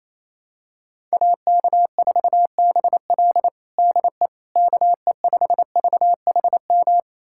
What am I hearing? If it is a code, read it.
Morse code, heard as a keyed tone, about 28 wpm, 700 Hz. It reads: AK4BL DE KE5VHM
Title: USA Callsign DE Callsign 28wpm